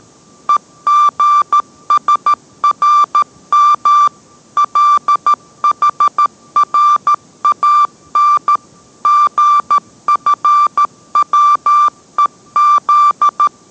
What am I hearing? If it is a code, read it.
Morse code, heard as a keyed tone, about 13 wpm, 1180 Hz. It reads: EGSRM LHRAN GFWEZ